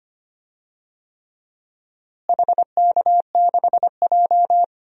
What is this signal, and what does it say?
Morse code, heard as a keyed tone, about 25 wpm, 700 Hz. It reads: HK6J
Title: DXCC-100 Call Signs - 4 Characters - Repeated 3x - 25wpm